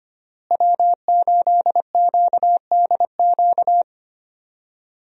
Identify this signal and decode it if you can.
Morse code, heard as a keyed tone, about 25 wpm, 700 Hz. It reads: W8QDQ
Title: USA Call Signs - 5 Characters - Repeated 2x - 25wpm